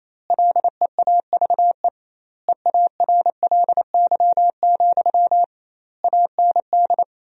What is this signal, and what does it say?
Morse code, heard as a keyed tone, about 28 wpm, 700 Hz. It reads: LEAVE EARLY, AND
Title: 2x Word-Spacing - Sentences from Top 400 Words - 28wpm